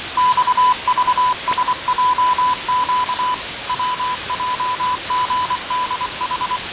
Morse code, heard as a keyed tone, about 24 wpm, 1000 Hz, with radio noise. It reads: XVSJQ WJGDH
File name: fav22-6824usb-20251029-1420z-voice-test-win10